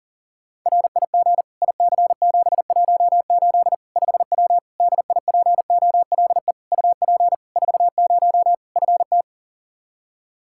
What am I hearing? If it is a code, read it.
Morse code, heard as a keyed tone, about 37 wpm, 700 Hz. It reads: RIG IC718 5W DIPOLE UP 40 FT
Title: K1USN SST QSO Practice 40wpm